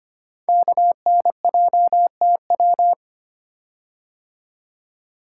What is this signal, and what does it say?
Morse code, heard as a keyed tone, about 25 wpm, 700 Hz. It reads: KNJTW
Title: Code Groups - 5 Letter:Numbers - 25wpm